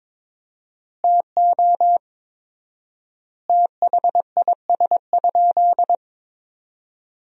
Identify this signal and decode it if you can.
Morse code, heard as a keyed tone, about 21 wpm, 700 Hz. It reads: TO THIS?
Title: 4x Word-Spacing - Sentences from Top 100 Words - 22wpm